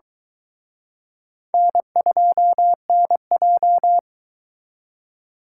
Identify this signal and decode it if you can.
Morse code, heard as a keyed tone, about 23 wpm, 700 Hz. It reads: N2NJ